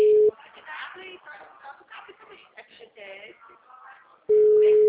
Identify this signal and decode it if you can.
Ringback tone (Europe, 425 Hz), with speech.
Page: Atibaia - SP 1